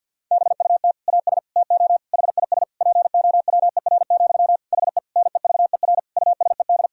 Morse code, heard as a keyed tone, about 50 wpm, 700 Hz. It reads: BUT AS TO HIS POWER, HE NEVER USED
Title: Sentences from Top 1000 Words - Review Part 2 - 50wpm